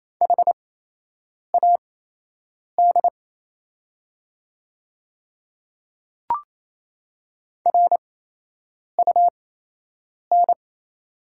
Morse code, heard as a keyed tone, about 28 wpm, spaced as for 8 wpm, 700 Hz. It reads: HAD RUN
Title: 8x Character Spacing - Top 500 Words - 28wpm